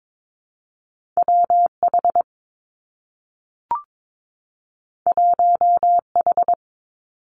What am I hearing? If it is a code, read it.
Morse code, heard as a keyed tone, about 22 wpm, 700 Hz. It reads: W H 1 H
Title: Two Challenging Letter-Number Combinations 22wpm